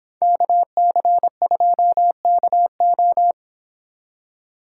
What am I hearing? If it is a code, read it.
Morse code, heard as a keyed tone, about 26 wpm, 700 Hz. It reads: KC2KO
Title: Speed Racing - USA 2x2 Call Signs 22wpm